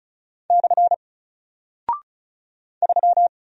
Morse code, heard as a keyed tone, about 36 wpm, 700 Hz. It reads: / 3
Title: Single Character (Letter:Number:Punctuation:Prosign) Alt 35wpm